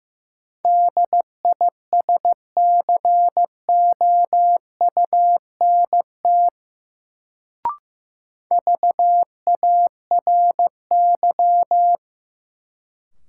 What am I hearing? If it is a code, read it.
Morse code, heard as a keyed tone, about 15 wpm, 700 Hz. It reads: DISCOUNT VARY
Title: Sentences from Top 1200 Words - New Word Drill - 15wpm